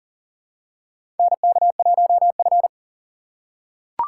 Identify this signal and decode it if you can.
Morse code, heard as a keyed tone, about 40 wpm, 700 Hz. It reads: NK1F
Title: USA Call Signs - 4 Characters - 6x Character Spacing - 40wpm